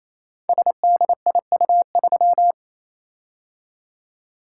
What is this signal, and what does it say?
Morse code, heard as a keyed tone, about 28 wpm, 700 Hz. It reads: SDIU3
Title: Code Groups - 5 Letter:Numbers - 28wpm